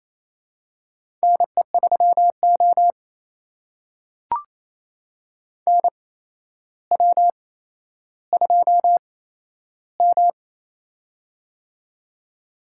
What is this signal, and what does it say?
Morse code, heard as a keyed tone, about 28 wpm, spaced as for 12 wpm, 700 Hz. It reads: NE3O NW2M
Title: USA Call Signs - 4 Characters - 8x Character Spacing - 28wpm